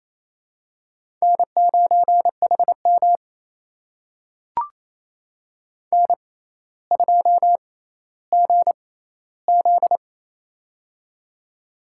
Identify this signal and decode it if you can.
Morse code, heard as a keyed tone, about 28 wpm, spaced as for 14 wpm, 700 Hz. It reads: N9HM N2GZ
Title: USA Call Signs - 4 Characters - 6x Character Spacing - 28wpm